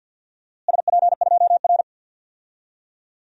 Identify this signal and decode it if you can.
Morse code, heard as a keyed tone, about 46 wpm, 700 Hz. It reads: SP2R
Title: DXCC-100 Call Signs - 4 Characters - Repeated 2x - 50wpm